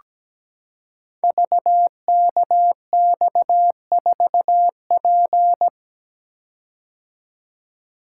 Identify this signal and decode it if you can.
Morse code, heard as a keyed tone, about 17 wpm, 700 Hz. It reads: VKX4P
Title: Code Groups - 5 Letter:Numbers - 17wpm